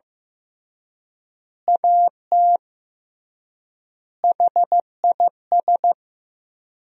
Morse code, heard as a keyed tone, about 16 wpm, 700 Hz. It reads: AT HIS
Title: 3x Word-Spacing - Sentences from Top 300 Words - 15wpm